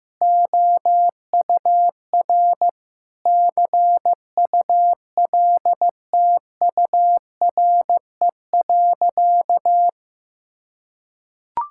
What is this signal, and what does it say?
Morse code, heard as a keyed tone, about 14 wpm, 700 Hz. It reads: OUR CULTURE.
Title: Sentences from Top 1000 Words - Review Part 6 - 15wpm